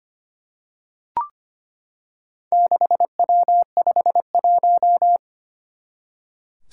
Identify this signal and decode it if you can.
Morse code, heard as a keyed tone, about 25 wpm, 700 Hz. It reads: 6W51
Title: Four Challenging Letter-Number Combinations 25wpm